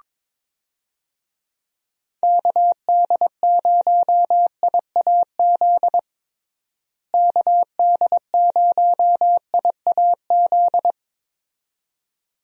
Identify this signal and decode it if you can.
Morse code, heard as a keyed tone, about 22 wpm, 700 Hz. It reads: KD0IAZ KD0IAZ
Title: USA Call Signs - 6 Characters - Repeated 2x - 22wpm